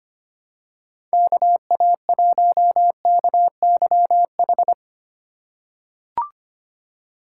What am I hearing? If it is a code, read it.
Morse code, heard as a keyed tone, about 25 wpm, 700 Hz. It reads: KA1KYH
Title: USA Call Signs - 6 Characters - 2x Character Spacing - 25wpm